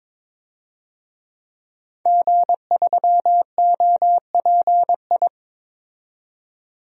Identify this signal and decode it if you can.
Morse code, heard as a keyed tone, about 22 wpm, 700 Hz. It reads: G3OPI
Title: DXCC-100 Call Signs - 5 Characters - 22wpm